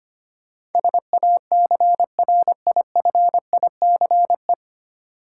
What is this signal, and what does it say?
Morse code, heard as a keyed tone, about 25 wpm, 700 Hz. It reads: SACRIFICE
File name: Nine Letter Words - 25wpm